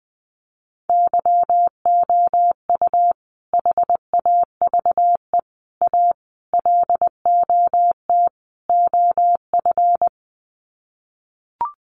Morse code, heard as a keyed tone, about 20 wpm, 700 Hz. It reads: YOU HAVE A LOT OF
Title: Sets of 5 Words Encore 20wpm